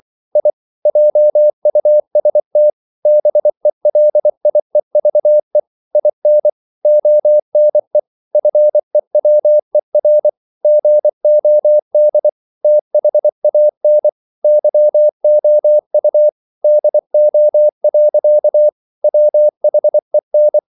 Morse code, heard as a keyed tone, about 24 wpm, 600 Hz. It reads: I JUST BELIEVE IN ONE FEWER GOD THAN YOU DO. WHEN
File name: CW / Practice / EnglishQuotations0424